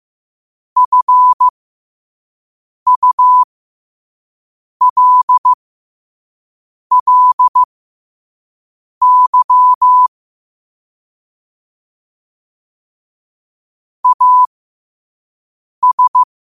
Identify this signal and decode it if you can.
Morse code, heard as a keyed tone, about 15 wpm, spaced as for 5 wpm, 1000 Hz. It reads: FULLY AS
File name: qotdfiles / 5wpm